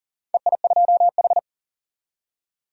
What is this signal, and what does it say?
Morse code, heard as a keyed tone, about 40 wpm, 700 Hz. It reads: EI2H
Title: DXCC-100 Call Signs - 4 Characters - 40wpm